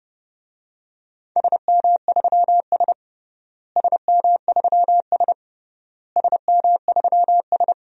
Morse code, heard as a keyed tone, about 32 wpm, 700 Hz. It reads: SM3S SM3S SM3S
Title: DXCC-100 Call Signs - 4 Characters - Repeated 3x - 30wpm